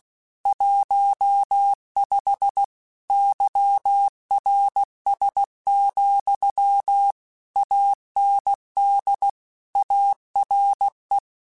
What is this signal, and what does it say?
Morse code, heard as a keyed tone, about 16 wpm, 800 Hz. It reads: 15 YRS, AND ARE
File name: QSOs / QSOs-mp3s / WD0BGP 1@15